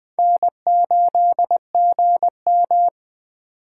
Morse code, heard as a keyed tone, about 24 wpm, 700 Hz. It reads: N8GM
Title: Speed Racing - USA 1x2 Call Signs 20wpm